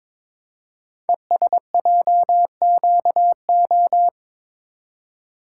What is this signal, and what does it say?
Morse code, heard as a keyed tone, about 22 wpm, 700 Hz. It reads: ESJQO
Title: Code Groups - 5 Letters - 22wpm